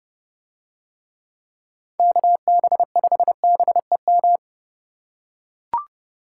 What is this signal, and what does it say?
Morse code, heard as a keyed tone, about 29 wpm, 700 Hz. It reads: KB5BEM